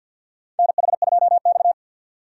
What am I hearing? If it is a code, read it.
Morse code, heard as a keyed tone, about 50 wpm, 700 Hz. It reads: NH2X
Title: USA Call Signs - 4 Characters - 6x Character Spacing - 50wpm